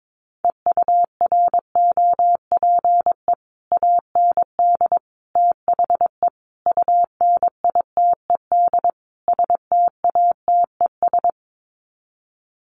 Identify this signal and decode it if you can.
Morse code, heard as a keyed tone, about 22 wpm, 700 Hz. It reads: EUROPE AND THE UNITED STATES
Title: Sets of 5 Words Encore2 22wpm